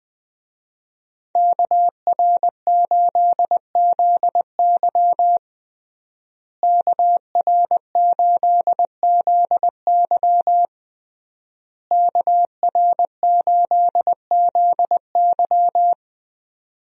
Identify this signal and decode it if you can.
Morse code, heard as a keyed tone, about 21 wpm, 700 Hz. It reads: KR8ZY KR8ZY KR8ZY
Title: USA Call Signs - 5 Characters - Repeated 3x - 20wpm